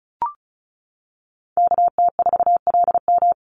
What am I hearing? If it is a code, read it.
Morse code, heard as a keyed tone, about 34 wpm, 700 Hz. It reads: KT4LM
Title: USA Call Signs 35wpm